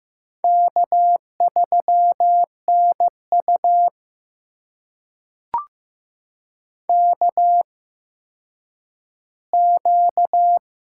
Morse code, heard as a keyed tone, about 15 wpm, spaced as for 9 wpm, 700 Hz. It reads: K3NU KQ
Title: USA Call Signs - 4 Characters - 8x Character Spacing - 15wpm